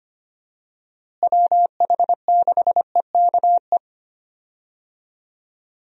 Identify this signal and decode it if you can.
Morse code, heard as a keyed tone, about 25 wpm, 700 Hz. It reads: WH6EKE
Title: USA Call Signs - 6 Characters - Repeated 2x - 25wpm